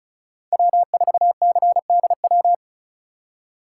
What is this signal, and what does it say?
Morse code, heard as a keyed tone, about 34 wpm, 700 Hz. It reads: W4CDW
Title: USA Call Signs - 5 Characters - 4x Character Spacing - 35wpm